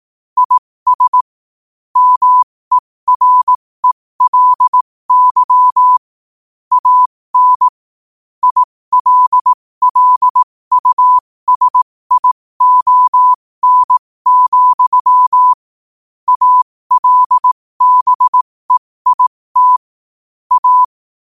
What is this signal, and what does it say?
Morse code, heard as a keyed tone, about 18 wpm, spaced as for 15 wpm, 1000 Hz. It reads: IS MERELY AN ILLUSION, ALBEIT A
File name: qotdfiles / 15wpm